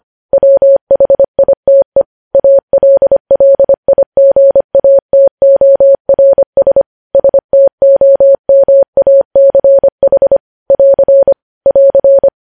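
Morse code, heard as a keyed tone, about 25 wpm, 550 Hz. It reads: WHITE ALLIGATORS STOMACH <AR> <AR>
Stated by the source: Morse code news headlines sent at 25 wpm.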